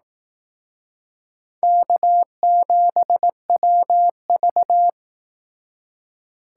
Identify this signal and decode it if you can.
Morse code, heard as a keyed tone, about 18 wpm, 700 Hz. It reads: K7WV
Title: Speed Racing - USA Call Signs 15wpm